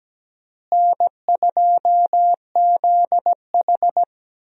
Morse code, heard as a keyed tone, about 17 wpm, 700 Hz. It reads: N2ZH
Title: USA Call Signs - 4 Characters - Repeated 3x - 17wpm